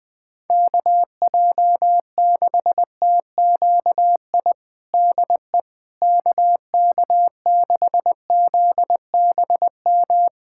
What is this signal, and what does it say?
Morse code, heard as a keyed tone, about 20 wpm, 690 Hz. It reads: KJ6TQI DE KK6ZBM
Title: USA Callsign DE Callsign 20wpm